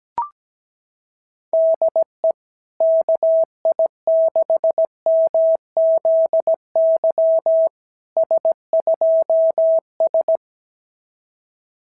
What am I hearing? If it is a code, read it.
Morse code, heard as a keyed tone, about 15 wpm, 650 Hz. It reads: DE KI6MZY S2S
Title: SOTA QSO Practice 17wpm